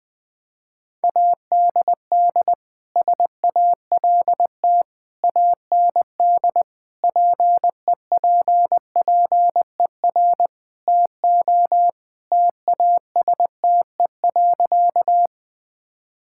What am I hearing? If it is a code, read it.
Morse code, heard as a keyed tone, about 20 wpm, 700 Hz. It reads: ADD SALT AND PEPPER TO TASTE.
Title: Sentences with 5-Word Phrases - Spoken Beginning - 20wpm